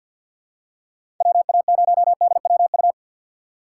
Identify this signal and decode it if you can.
Morse code, heard as a keyed tone, about 51 wpm, 700 Hz. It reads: W A 0 D W U
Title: USA Call Signs - 6 Characters - 6x Character Spacing - 50wpm